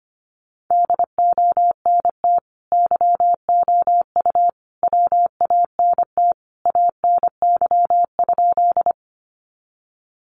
Morse code, heard as a keyed tone, about 25 wpm, 700 Hz. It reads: DONT YOU WANT ANY?
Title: Sentences from Top 300 Words 25wpm